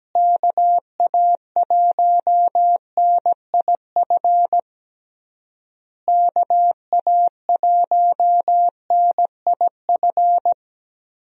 Morse code, heard as a keyed tone, about 17 wpm, 700 Hz. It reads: KA1NIF KA1NIF
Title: USA Call Signs - 6 Characters - Repeated 3x - 17wpm